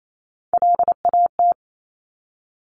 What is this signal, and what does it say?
Morse code, heard as a keyed tone, about 28 wpm, 700 Hz. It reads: LAT
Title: Top Three Letter Combinations in Words 28wpm